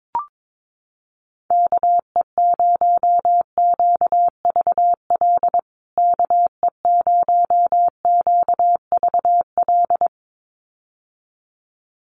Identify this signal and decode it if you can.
Morse code, heard as a keyed tone, about 22 wpm, 700 Hz. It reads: KE0QVL KE0QVL
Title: USA Call Signs Repeated 2x 22wpm